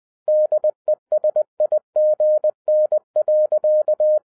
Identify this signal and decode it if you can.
Morse code, heard as a keyed tone, about 20 wpm, 610 Hz. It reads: DESIGN.
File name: 2011-07-mp3 / n43 Czech Army Morse Keys 20wpm